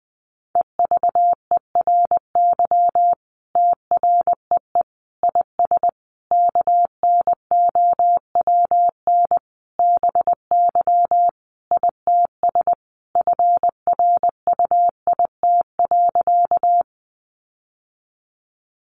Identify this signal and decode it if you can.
Morse code, heard as a keyed tone, about 20 wpm, 700 Hz. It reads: EVERY TREE IS KNOWN BY ITS FRUIT.
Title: Sentences from Top 1300 Words 20wpm